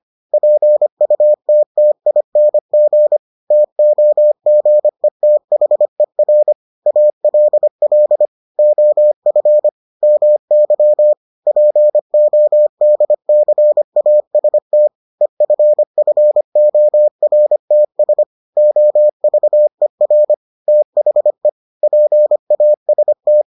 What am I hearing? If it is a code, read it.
Morse code, heard as a keyed tone, about 25 wpm, 600 Hz. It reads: PUTTING TOGETHER ALL OF MY PODCAST EFFORTS OVER THE PAST